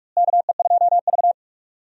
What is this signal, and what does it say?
Morse code, heard as a keyed tone, about 46 wpm, 700 Hz. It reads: KE2V
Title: USA Call Signs - 4 Characters - Repeated 2x - 45wpm